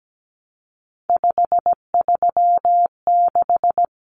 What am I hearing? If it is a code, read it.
Morse code, heard as a keyed tone, about 17 wpm, 700 Hz. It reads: 536
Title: Numbers - 3-Digit 17wpm